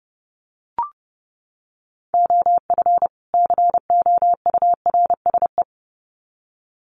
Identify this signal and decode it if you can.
Morse code, heard as a keyed tone, about 30 wpm, 700 Hz. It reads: OF COURSE
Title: Sets of 2 Words 30wpm